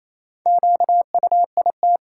Morse code, heard as a keyed tone, about 28 wpm, 700 Hz. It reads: QUIT
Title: Four Letter Words - 28wpm